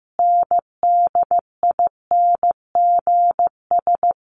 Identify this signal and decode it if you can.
Morse code, heard as a keyed tone, about 14 wpm, 700 Hz. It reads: NDINGS
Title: Sets of 5 Words Encore2 15wpm